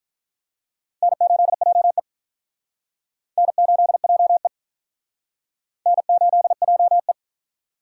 Morse code, heard as a keyed tone, about 46 wpm, 700 Hz. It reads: N8JE N8JE N8JE